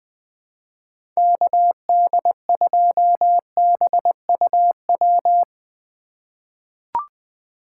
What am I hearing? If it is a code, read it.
Morse code, heard as a keyed tone, about 20 wpm, 700 Hz. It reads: KD2BUW
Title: USA Call Signs - 2x Character Spacing - 20wpm